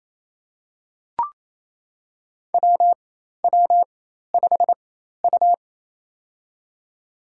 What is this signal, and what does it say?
Morse code, heard as a keyed tone, about 28 wpm, spaced as for 13 wpm, 700 Hz. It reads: WW5U
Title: USA Call Signs - 4 Characters - 4x Character Spacing - 28wpm